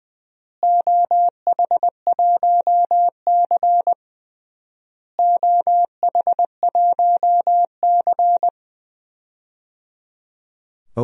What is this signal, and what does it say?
Morse code, heard as a keyed tone, about 20 wpm, 700 Hz. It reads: OH1C OH1C
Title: DXCC-100 Call Signs - 4 Characters - Repeated 3x - 20wpm